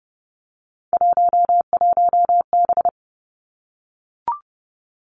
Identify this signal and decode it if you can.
Morse code, heard as a keyed tone, about 30 wpm, 700 Hz. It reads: 11B
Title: Three Challenging Letter-Number Combinations 30wpm